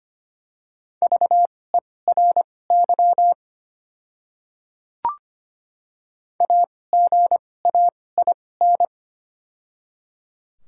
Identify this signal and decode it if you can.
Morse code, heard as a keyed tone, about 25 wpm, spaced as for 18 wpm, 700 Hz. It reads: VERY AGAIN
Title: 2x Character Spacing - Top 300 Words - 25wpm